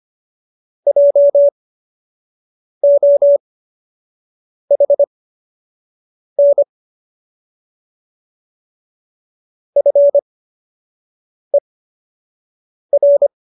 Morse code, heard as a keyed tone, about 25 wpm, spaced as for 6 wpm, 570 Hz. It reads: JOHN FER